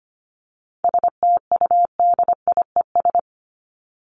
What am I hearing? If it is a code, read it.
Morse code, heard as a keyed tone, about 25 wpm, 700 Hz. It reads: STUDIES
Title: ES:S Suffix - Top 100 Words with Base Words 25wpm